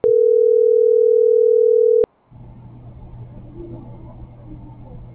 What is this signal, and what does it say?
Ringback tone (North America), 440 + 480 Hz.